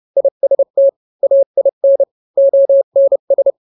Morse code, heard as a keyed tone, about 30 wpm, 550 Hz. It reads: IST AIN ONS